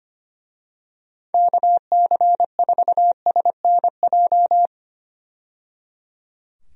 Morse code, heard as a keyed tone, about 25 wpm, 700 Hz. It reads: KC4SNJ
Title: USA Call Signs - 6 Characters - 25wpm